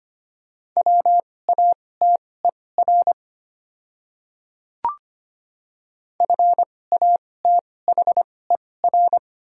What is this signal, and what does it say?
Morse code, heard as a keyed tone, about 25 wpm, spaced as for 18 wpm, 700 Hz. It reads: WATER FATHER
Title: 2x Character Spacing - Top 500 Words - 25wpm